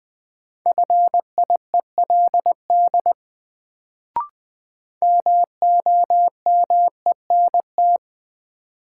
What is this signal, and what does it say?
Morse code, heard as a keyed tone, about 20 wpm, 700 Hz. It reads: FIELD MOMENT
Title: Sentences from Top 1000 Words - New Word Drill - 20wpm